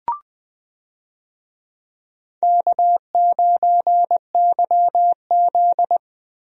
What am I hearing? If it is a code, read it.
Morse code, heard as a keyed tone, about 20 wpm, 700 Hz. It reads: K9YZ
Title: USA Call Signs - 4 Characters - 20wpm